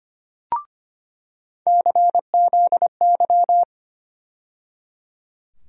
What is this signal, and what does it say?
Morse code, heard as a keyed tone, about 25 wpm, 700 Hz. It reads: CZY
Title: USA Call Sign Postfixes - 3 Letters 25wpm